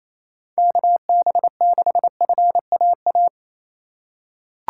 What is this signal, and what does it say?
Morse code, heard as a keyed tone, about 28 wpm, 700 Hz. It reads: KB6FAA
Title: USA Call Signs - 2x Character Spacing - 28wpm